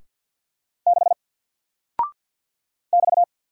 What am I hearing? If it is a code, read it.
Morse code, heard as a keyed tone, about 50 wpm, 700 Hz. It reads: 6 <BT>
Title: Single Character (Letter:Number:Punctuation:Prosign) Alt 50wpm